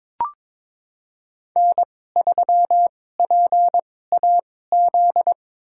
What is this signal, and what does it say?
Morse code, heard as a keyed tone, about 22 wpm, spaced as for 16 wpm, 700 Hz. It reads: N3PAZ
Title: USA Call Signs - 5 Characters - 2x Character Spacing - 22wpm